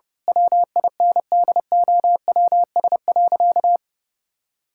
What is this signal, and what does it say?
Morse code, heard as a keyed tone, about 30 wpm, 700 Hz. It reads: WINDOWS.
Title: Pangram Challenge v2 - 30wpm